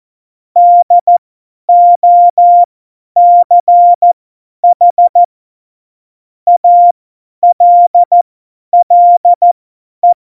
Morse code, heard as a keyed tone, about 14 wpm, spaced as for 10 wpm, 700 Hz. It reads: DOCH ALLE
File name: Morsekurs / Witze / Witz 22. Oktober 2019, 50 F70